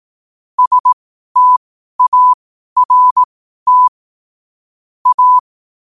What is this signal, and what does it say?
Morse code, heard as a keyed tone, about 18 wpm, spaced as for 12 wpm, 1000 Hz. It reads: START A
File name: qotdfiles / 12wpm